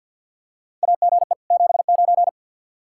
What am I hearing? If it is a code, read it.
Morse code, heard as a keyed tone, about 46 wpm, 700 Hz. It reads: AGE 79
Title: Practice QSO - Exchange 3 - More Info - 50wpm